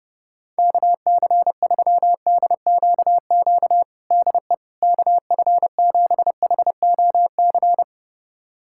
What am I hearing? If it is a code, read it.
Morse code, heard as a keyed tone, about 30 wpm, 700 Hz. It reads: KC3DQQ DE KF7HOC
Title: USA Callsign DE Callsign 30wpm